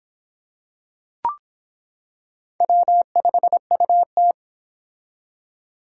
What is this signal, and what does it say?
Morse code, heard as a keyed tone, about 26 wpm, 700 Hz. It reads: W5UT